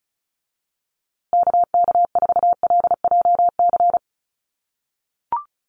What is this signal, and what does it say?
Morse code, heard as a keyed tone, about 34 wpm, 700 Hz. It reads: KK4LJC
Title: USA Call Signs Repeated 3x 35wpm